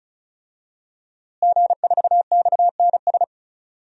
Morse code, heard as a keyed tone, about 34 wpm, 700 Hz. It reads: G4XNS
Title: DXCC-100 Call Signs - 5 Characters - Repeated 2x - 35wpm